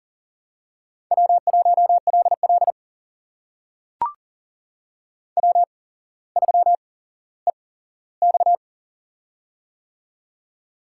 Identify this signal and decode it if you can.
Morse code, heard as a keyed tone, about 44 wpm, spaced as for 18 wpm, 700 Hz. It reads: W1LL W3EX
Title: USA Call Signs - 4 Characters - 8x Character Spacing - 40wpm